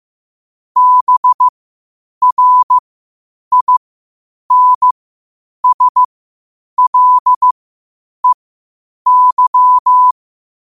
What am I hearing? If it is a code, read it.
Morse code, heard as a keyed tone, about 15 wpm, spaced as for 8 wpm, 1000 Hz. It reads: BRINSLEY
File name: qotdfiles / 8wpm